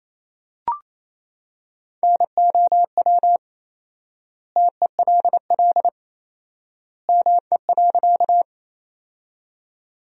Morse code, heard as a keyed tone, about 27 wpm, 700 Hz. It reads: NOW TELL ME.
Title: 4x Word-Spacing - Sentences from Top 200 Words - 28wpm